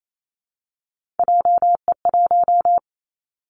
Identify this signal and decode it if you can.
Morse code, heard as a keyed tone, about 28 wpm, 700 Hz. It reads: JE1
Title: Callsign Prefixes for Top 25 DXCC - 28wpm